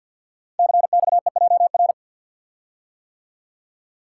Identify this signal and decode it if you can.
Morse code, heard as a keyed tone, about 51 wpm, 700 Hz. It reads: CXEJR
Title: Code Groups - 5 Letter:Numbers - 50wpm